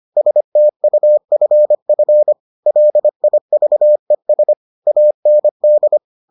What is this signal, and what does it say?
Morse code, heard as a keyed tone, about 25 wpm, 600 Hz. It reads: STUFF LIVES AND